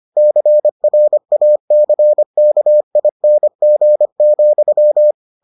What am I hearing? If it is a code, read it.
Morse code, heard as a keyed tone, about 25 wpm, 600 Hz. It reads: CRACKING,